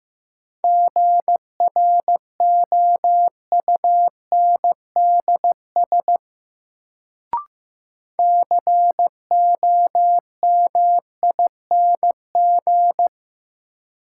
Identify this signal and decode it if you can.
Morse code, heard as a keyed tone, about 15 wpm, 700 Hz. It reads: GROUNDS COMING
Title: Sentences from Top 1300 Words - New Word Drill Alt - 15wpm